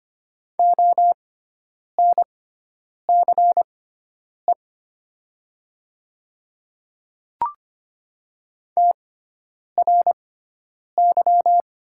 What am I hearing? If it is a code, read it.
Morse code, heard as a keyed tone, about 25 wpm, spaced as for 9 wpm, 700 Hz. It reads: ONCE TRY
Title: 6x Character Spacing - Top 300 Words - 25wpm